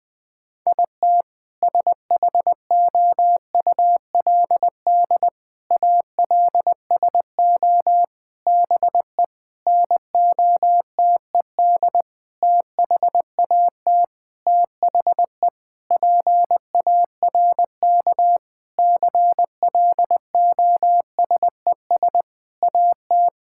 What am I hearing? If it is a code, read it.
Morse code, heard as a keyed tone, about 20 wpm, 700 Hz. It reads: IT SHOULD ALSO BE NOTED THAT THE PARK CLOSES AT
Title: Sentences with 6-Word Phrases - 20wpm